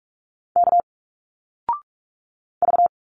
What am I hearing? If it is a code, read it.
Morse code, heard as a keyed tone, about 46 wpm, 700 Hz. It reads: K V
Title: Single Letters 45wpm